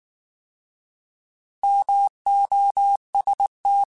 Morse code, heard as a keyed tone, about 19 wpm, 800 Hz. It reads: MOST
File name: Alphabetical / Alphabetical mp3s / Ms @18